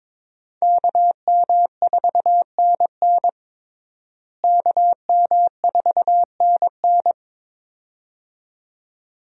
Morse code, heard as a keyed tone, about 22 wpm, 700 Hz. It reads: KM4NN KM4NN
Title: USA Call Signs - 5 Characters - Repeated 2x - 22wpm